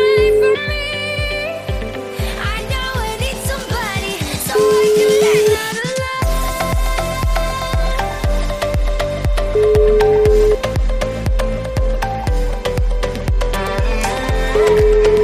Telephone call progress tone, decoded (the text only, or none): ringback tone